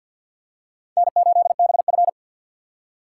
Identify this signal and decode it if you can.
Morse code, heard as a keyed tone, about 52 wpm, 700 Hz. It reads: N8BF
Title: USA Call Signs - 4 Characters - 2x Character Spacing - 50wpm